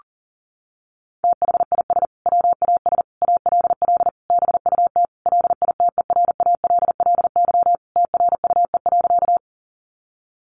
Morse code, heard as a keyed tone, about 40 wpm, 700 Hz. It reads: THIS WAS ALL BUT LITERALLY TRUE.
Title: Sentences from Top 1400 Words 40wpm